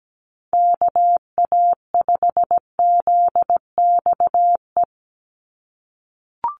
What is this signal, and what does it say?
Morse code, heard as a keyed tone, about 17 wpm, 700 Hz. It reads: KA5ZXE
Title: USA Call Signs 17wpm